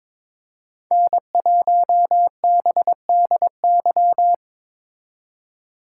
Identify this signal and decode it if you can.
Morse code, heard as a keyed tone, about 27 wpm, 700 Hz. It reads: N1BDY
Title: Speed Racing - USA Call Signs 22wpm